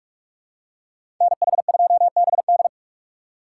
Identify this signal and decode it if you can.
Morse code, heard as a keyed tone, about 42 wpm, 700 Hz. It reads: NH2BD
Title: USA Call Signs - 5 Characters - 3x Character Spacing - 45wpm